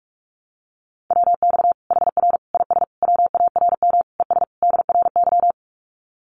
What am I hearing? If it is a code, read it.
Morse code, heard as a keyed tone, about 44 wpm, 700 Hz. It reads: WX HR IS WARM ES DRY
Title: Common QSO Phrases 45wpm